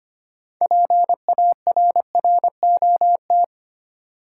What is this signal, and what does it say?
Morse code, heard as a keyed tone, about 25 wpm, 690 Hz. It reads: PARROT
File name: practice / 25wpm / Easy Words with Hints - 25wpm